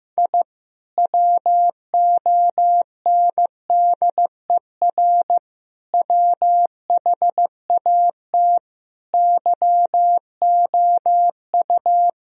Morse code, heard as a keyed tone, about 15 wpm, 700 Hz. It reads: I WONDER WHAT YOU
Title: Sentences from Top 1000 Words - Review Part 3 - 15wpm